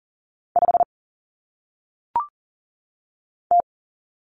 Morse code, heard as a keyed tone, about 40 wpm, 700 Hz. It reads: 5 T
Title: Common Cut Numbers - 1-Digit 40wpm